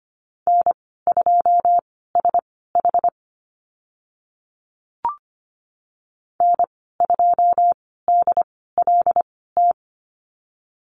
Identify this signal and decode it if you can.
Morse code, heard as a keyed tone, about 25 wpm, spaced as for 16 wpm, 690 Hz. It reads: N2SH N2DLT
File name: Lesson 15g - Number 5 - Call Signs 25wpm